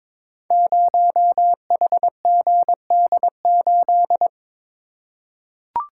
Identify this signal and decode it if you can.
Morse code, heard as a keyed tone, about 22 wpm, 700 Hz. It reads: 0HGD8
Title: Code Groups - 5 Letter:Numbers - 22wpm